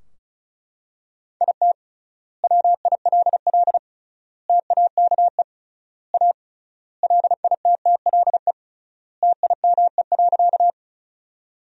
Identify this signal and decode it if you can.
Morse code, heard as a keyed tone, about 34 wpm, 700 Hz. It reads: IT WILL TAKE A LITTLE TIME.
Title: 3x Word-Spacing - Sentences from Top 200 Words - 35wpm